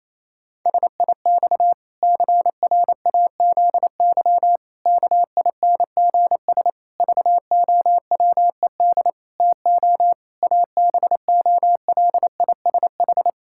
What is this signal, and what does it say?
Morse code, heard as a keyed tone, about 28 wpm, 700 Hz. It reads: SIX CRAZY KINGS VOWED TO ABOLISH
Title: Pangram Challenge v2 - 28wpm